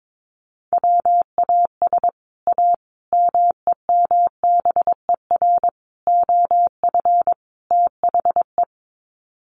Morse code, heard as a keyed tone, about 22 wpm, 700 Hz. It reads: WAS A MEMBER OF THE
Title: Sets of 5 Words 22wpm